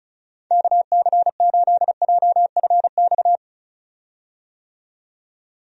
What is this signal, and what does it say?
Morse code, heard as a keyed tone, about 34 wpm, 700 Hz. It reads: KC8JFX
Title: USA Call Signs - 6 Characters - 35wpm